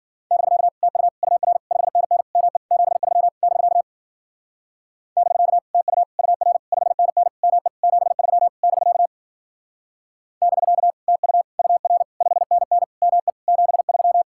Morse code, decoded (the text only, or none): <BK> TU UR 5NN ME 73 <BK> <BK> TU UR 5NN ME 73 <BK> <BK> TU UR 5NN ME 73